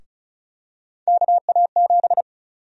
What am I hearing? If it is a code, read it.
Morse code, heard as a keyed tone, about 36 wpm, 700 Hz. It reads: KA7
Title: USA Call Sign Prefixes - 35wpm